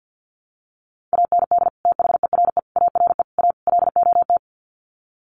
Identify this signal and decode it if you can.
Morse code, heard as a keyed tone, about 50 wpm, 700 Hz. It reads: AND THERE ARE A LOT
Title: Sets of 5 Words Encore2 50wpm